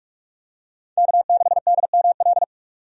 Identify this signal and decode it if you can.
Morse code, heard as a keyed tone, about 47 wpm, 700 Hz. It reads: K6DML
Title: USA Call Signs - 5 Characters - 45wpm